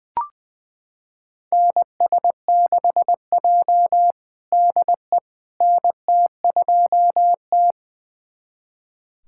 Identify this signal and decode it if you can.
Morse code, heard as a keyed tone, about 20 wpm, 700 Hz. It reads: NS6J DE NT2T
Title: USA Callsign DE Callsign - 4 Characters - 20wpm